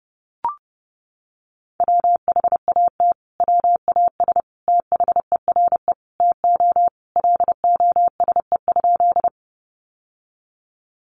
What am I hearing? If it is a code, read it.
Morse code, heard as a keyed tone, about 28 wpm, 700 Hz. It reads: WHAT WAS THERE TO LOSE?
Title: Sentences from Top 1100 Words 30wpm